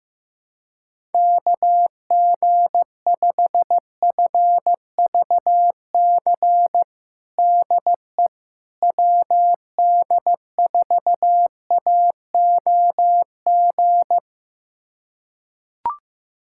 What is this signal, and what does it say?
Morse code, heard as a keyed tone, about 15 wpm, 700 Hz. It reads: KG5FVC DE WD4AOG
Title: USA Callsign DE Callsign 15wpm